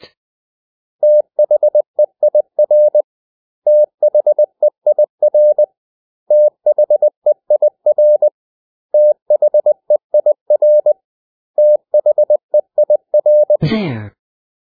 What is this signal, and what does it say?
Morse code, heard as a keyed tone, about 20 wpm, 600 Hz. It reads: THEIR THEIR THEIR THEIR THEIR
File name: cw / 100 Common English Words